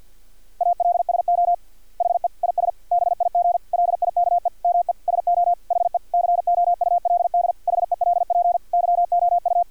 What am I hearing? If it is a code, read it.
Morse code, heard as a keyed tone, about 50 wpm, 700 Hz. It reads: ALSO HE IS BIG LIKE ME SO HE COULD HELP YOU
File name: archived / CRSnet